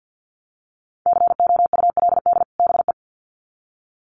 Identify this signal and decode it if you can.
Morse code, heard as a keyed tone, about 50 wpm, 700 Hz. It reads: COULD BE
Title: Sets of 2 Words 50wpm